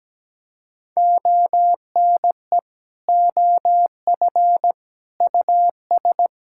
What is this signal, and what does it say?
Morse code, heard as a keyed tone, about 17 wpm, 700 Hz. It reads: ONE OF US
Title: Sentences from Top 1600 Words 17wpm